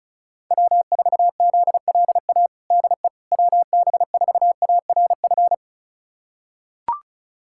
Morse code, heard as a keyed tone, about 36 wpm, 700 Hz. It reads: W4ZLA DE WB4ARF
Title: USA Callsign DE Callsign 35wpm